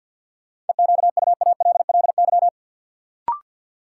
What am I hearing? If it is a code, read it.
Morse code, heard as a keyed tone, about 53 wpm, 700 Hz. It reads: EQUALLY
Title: Sentences from Top 1400 Words - New Word Drill Alt - 50wpm